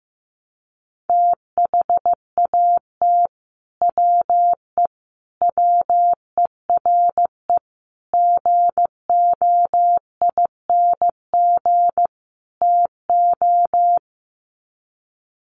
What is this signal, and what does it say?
Morse code, heard as a keyed tone, about 15 wpm, 700 Hz. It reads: THAT WE WERE GOING TO
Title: Sets of 5 Words Encore2 15wpm